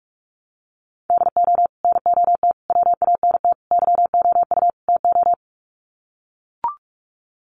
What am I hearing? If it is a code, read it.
Morse code, heard as a keyed tone, about 44 wpm, 700 Hz. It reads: DO NOT WANT YOU TO
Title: Sets of 5 Words 45wpm